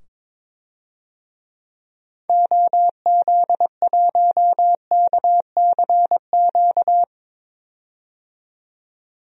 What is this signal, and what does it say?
Morse code, heard as a keyed tone, about 22 wpm, 700 Hz. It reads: OZ1KCQ
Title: DXCC-100 Call Signs - 6 Characters - 22wpm